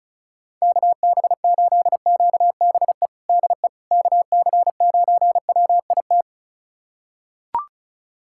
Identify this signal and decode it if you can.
Morse code, heard as a keyed tone, about 34 wpm, 700 Hz. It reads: KB8QBE DE KC9WIT
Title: USA Callsign DE Callsign 35wpm